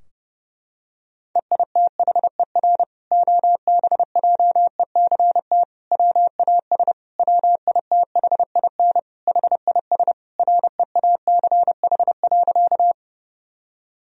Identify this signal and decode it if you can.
Morse code, heard as a keyed tone, about 32 wpm, 700 Hz. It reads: EITHER OBJECT WAS WITHIN HIS REACH.
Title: Sentences from Top 1000 Words - Review Part 4 - 30wpm